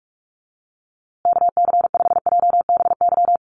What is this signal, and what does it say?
Morse code, heard as a keyed tone, about 44 wpm, 700 Hz. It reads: KC5JBY
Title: USA Call Signs Repeated 3x 45wpm